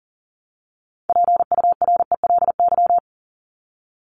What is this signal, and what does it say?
Morse code, heard as a keyed tone, about 40 wpm, 700 Hz. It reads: PURELY
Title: LY Suffix - Top 100 Words with Base Words 40wpm